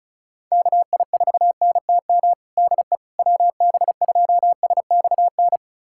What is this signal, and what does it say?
Morse code, heard as a keyed tone, about 36 wpm, 700 Hz. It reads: KI4NTM DE WB2SXN
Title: USA Callsign DE Callsign 35wpm